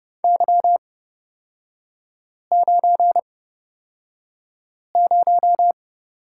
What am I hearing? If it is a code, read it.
Morse code, heard as a keyed tone, about 30 wpm, spaced as for 5 wpm, 700 Hz. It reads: Y90
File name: Lesson 37e - Letter x - Three Characters 30wpm